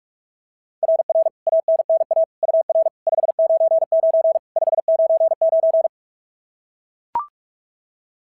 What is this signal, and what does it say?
Morse code, heard as a keyed tone, about 44 wpm, 690 Hz. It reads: RR ANNA UR 599 599
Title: SOTA QSO Practice 45wpm